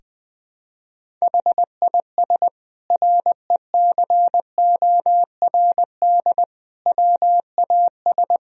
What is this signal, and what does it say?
Morse code, heard as a keyed tone, about 20 wpm, 700 Hz. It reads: HIS RECORD WAS
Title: Sentences from Top 1000 Words - Review Part 3 - 20wpm